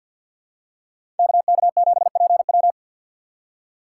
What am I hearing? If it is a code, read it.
Morse code, heard as a keyed tone, about 53 wpm, 700 Hz. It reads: KK7PW
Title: USA Call Signs - 5 Characters - 6x Character Spacing - 50wpm